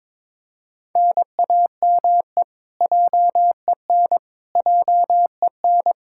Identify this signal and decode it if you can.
Morse code, heard as a keyed tone, about 21 wpm, 700 Hz. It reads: NAME JEN JEN
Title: Practice QSO - Exchange 1 - Beginning - 22wpm